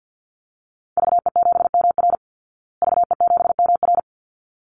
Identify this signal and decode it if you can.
Morse code, heard as a keyed tone, about 50 wpm, 700 Hz. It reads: VE7MR VE7MR
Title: Call Signs Repeated 3x - 50wpm